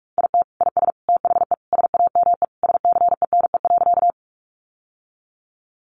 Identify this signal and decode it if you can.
Morse code, heard as a keyed tone, about 44 wpm, 700 Hz. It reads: IT IS THE SAME SCENE.
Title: Sentences from Top 1100 Words 45wpm